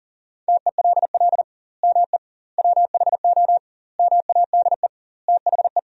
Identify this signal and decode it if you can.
Morse code, heard as a keyed tone, about 37 wpm, 700 Hz. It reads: TELL ME WHO MADE THE
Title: 2x Word-Spacing - Sentences from Top 200 Words - 40wpm